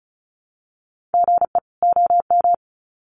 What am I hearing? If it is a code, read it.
Morse code, heard as a keyed tone, about 34 wpm, 700 Hz. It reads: GE OM
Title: Common QSO Phrases 35wpm